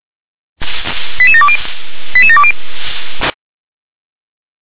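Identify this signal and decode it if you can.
Five-tone selective call (ZVEI), heard as quick runs of five tones, tones of about 70 ms. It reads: 88720 88720
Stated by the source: Five-tone ZVEI selective call received on SDR.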